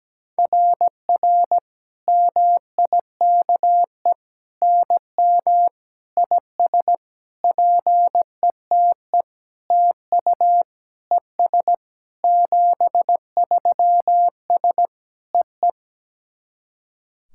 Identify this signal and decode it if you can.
Morse code, heard as a keyed tone, about 17 wpm, 700 Hz. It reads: RR MIKE NM IS PETE TU ES 73S EE
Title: SOTA QSO Practice 17wpm